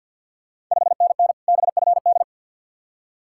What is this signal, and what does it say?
Morse code, heard as a keyed tone, about 52 wpm, 690 Hz. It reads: 5NN BFD
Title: Ontario QSO Party Exchange 50wpm